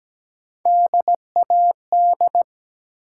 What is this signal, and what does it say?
Morse code, heard as a keyed tone, about 17 wpm, 700 Hz. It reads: DAD
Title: Three Letter Words - 17wpm